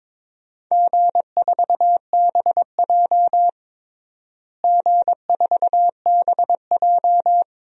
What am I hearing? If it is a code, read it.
Morse code, heard as a keyed tone, about 22 wpm, 700 Hz. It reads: G4BJ G4BJ
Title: DXCC-100 Call Signs - 4 Characters - Repeated 2x - 22wpm